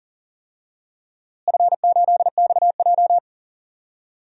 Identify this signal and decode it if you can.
Morse code, heard as a keyed tone, about 40 wpm, 700 Hz. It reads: F8XJ
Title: DXCC-100 Call Signs - 4 Characters - Repeated 2x - 40wpm